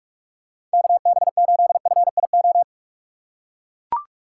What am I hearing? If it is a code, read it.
Morse code, heard as a keyed tone, about 41 wpm, 700 Hz. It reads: KB8FIO
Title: USA Call Signs - 6 Characters - 3x Character Spacing - 45wpm